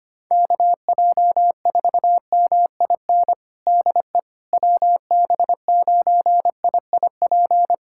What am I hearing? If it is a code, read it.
Morse code, heard as a keyed tone, about 25 wpm, 700 Hz. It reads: KJ4MIN DE WB9IIP
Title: USA Callsign DE Callsign - 6 Characters - 25wpm